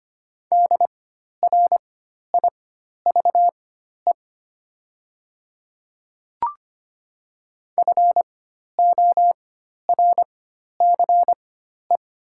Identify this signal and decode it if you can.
Morse code, heard as a keyed tone, about 25 wpm, spaced as for 12 wpm, 700 Hz. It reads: DRIVE FORCE
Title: 4x Character Spacing - Top 500 Words - 25wpm